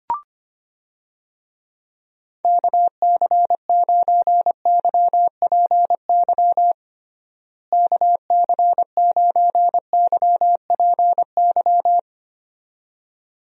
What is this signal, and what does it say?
Morse code, heard as a keyed tone, about 25 wpm, 700 Hz. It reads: KC9YPY KC9YPY
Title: USA Call Signs - 6 Characters - Repeated 2x - 25wpm